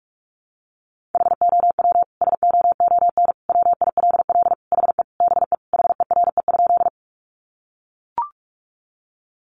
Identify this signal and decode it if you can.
Morse code, heard as a keyed tone, about 44 wpm, 700 Hz. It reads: HOW SOON WILL HE BE HERE?
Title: Sentences from Top 400 Words 45wpm